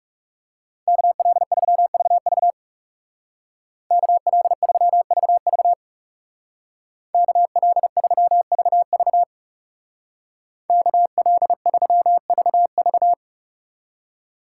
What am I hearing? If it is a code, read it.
Morse code, heard as a keyed tone, about 37 wpm, 700 Hz. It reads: KL3VV KL3VV KL3VV KL3VV
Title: Speed Racing - USA 2x2 Call Signs 30wpm